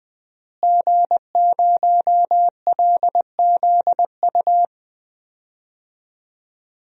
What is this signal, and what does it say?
Morse code, heard as a keyed tone, about 20 wpm, 700 Hz. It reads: G0LZU
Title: DXCC-100 Call Signs - 5 Characters - Repeated 3x - 20wpm